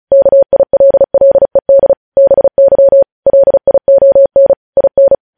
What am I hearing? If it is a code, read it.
Morse code, heard as a keyed tone, about 36 wpm, 550 Hz. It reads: KILLED BY LION IN
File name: podcast / News headlines-1x-35x35